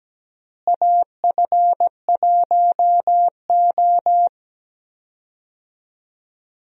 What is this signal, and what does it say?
Morse code, heard as a keyed tone, about 17 wpm, 700 Hz. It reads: AF1O
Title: USA Call Signs - 4 Characters - Repeated 2x - 17wpm